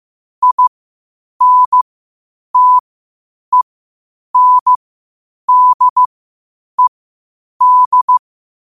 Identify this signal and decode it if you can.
Morse code, heard as a keyed tone, about 15 wpm, spaced as for 8 wpm, 1000 Hz. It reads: INTENDED
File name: qotdfiles / 8wpm